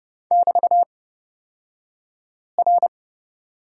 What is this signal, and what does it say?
Morse code, heard as a keyed tone, about 30 wpm, spaced as for 5 wpm, 700 Hz. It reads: <BT>R
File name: Lesson 36d - BT - Two Characters 30wpm